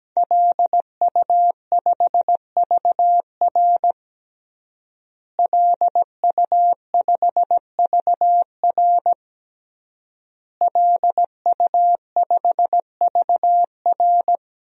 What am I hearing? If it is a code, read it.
Morse code, heard as a keyed tone, about 17 wpm, 700 Hz. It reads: LU5VR LU5VR LU5VR
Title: DXCC-100 Call Signs - 5 Characters - Repeated 3x - 17wpm